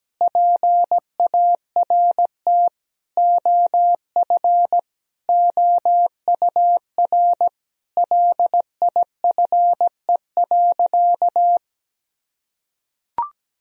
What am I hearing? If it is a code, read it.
Morse code, heard as a keyed tone, about 17 wpm, 700 Hz. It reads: PART OF OUR LIFE.
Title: Sentences from Top 1000 Words - Review Part 1 - 17wpm